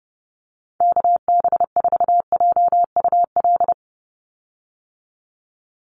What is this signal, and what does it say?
Morse code, heard as a keyed tone, about 29 wpm, 700 Hz. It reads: KB4JUL
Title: USA Call Signs 30wpm